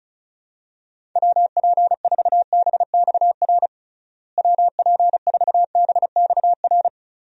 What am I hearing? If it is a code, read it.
Morse code, heard as a keyed tone, about 34 wpm, 700 Hz. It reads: WP4BXR WP4BXR
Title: USA Call Signs - 6 Characters - Repeated 2x - 35wpm